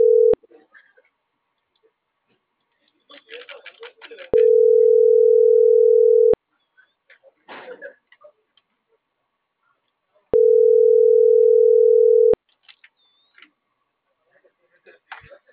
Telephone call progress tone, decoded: ringback tone

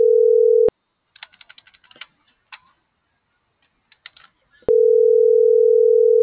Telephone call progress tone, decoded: ringback tone

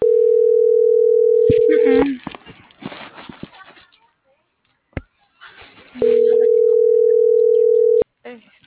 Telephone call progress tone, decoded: ringback tone

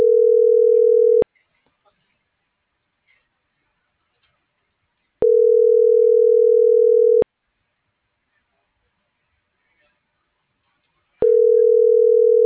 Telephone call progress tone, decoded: ringback tone

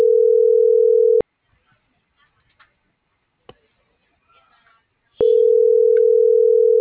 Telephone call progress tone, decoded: ringback tone

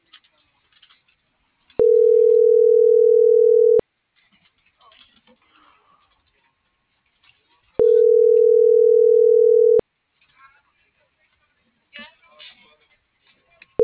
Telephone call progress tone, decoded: ringback tone

